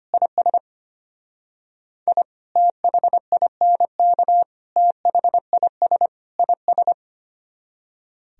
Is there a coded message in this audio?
Yes: Morse code from 0.1 to 6.9 s, heard as a keyed tone, about 32 wpm, 700 Hz. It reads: IS I THINK THIS IS